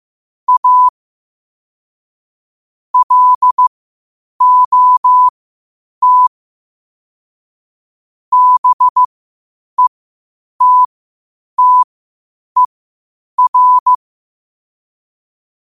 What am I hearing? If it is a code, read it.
Morse code, heard as a keyed tone, about 15 wpm, spaced as for 8 wpm, 1000 Hz. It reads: A LOT BETTER
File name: qotdfiles / 8wpm